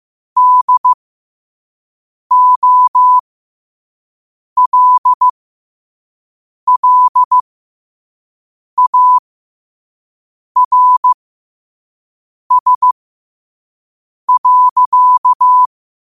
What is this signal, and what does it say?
Morse code, heard as a keyed tone, about 15 wpm, spaced as for 5 wpm, 1000 Hz. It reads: DOLLARS.
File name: qotdfiles / 5wpm